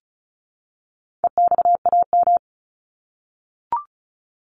Morse code, heard as a keyed tone, about 34 wpm, 700 Hz. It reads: EXAM
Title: Top 501-750 Words 35wpm